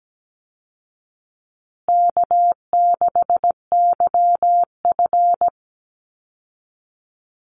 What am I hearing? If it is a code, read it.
Morse code, heard as a keyed tone, about 17 wpm, 700 Hz. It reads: K6YF
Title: Easy Call Signs Repeated 3x - 17wpm